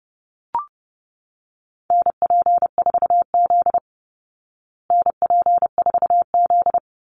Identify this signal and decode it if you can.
Morse code, heard as a keyed tone, about 30 wpm, 700 Hz. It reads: NP4Z NP4Z
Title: Call Signs Repeated 2x - 30wpm